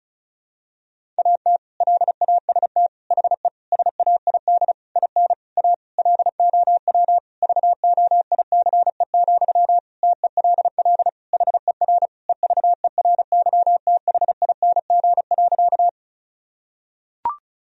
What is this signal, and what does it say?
Morse code, heard as a keyed tone, about 36 wpm, 700 Hz. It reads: AT LAST HE SAID IN A LOW VOICE, TELL HER EVERYTHING.
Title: Sentences from Top 1000 Words - Review Part 1 - 35wpm